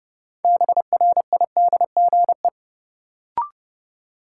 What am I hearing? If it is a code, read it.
Morse code, heard as a keyed tone, about 30 wpm, 700 Hz. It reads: BRIDGE
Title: Sentences from Top 1200 Words - New Word Drill Alt - 30wpm